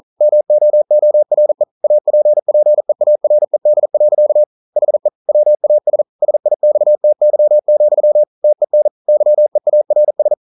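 Morse code, heard as a keyed tone, about 41 wpm, 600 Hz. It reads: MOORE APPEARED. HE WAS SIXTY, TEN YEARS
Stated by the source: Morse code practice at 41 wpm, from CW med Gnister.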